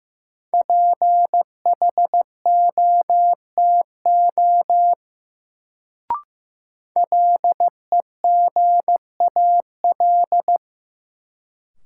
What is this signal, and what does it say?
Morse code, heard as a keyed tone, about 16 wpm, 700 Hz. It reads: PHOTO LEGAL
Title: Sentences from Top 1200 Words - New Word Drill - 15wpm